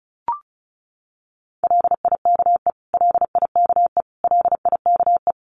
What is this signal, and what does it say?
Morse code, heard as a keyed tone, about 34 wpm, 700 Hz. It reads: LIKE LIKE LIKE
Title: Top 300 Words Repeated 35wpm